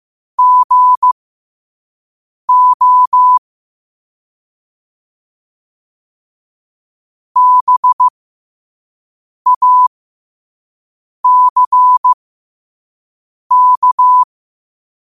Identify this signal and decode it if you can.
Morse code, heard as a keyed tone, about 15 wpm, spaced as for 5 wpm, 1000 Hz. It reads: GO BACK
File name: qotdfiles / 5wpm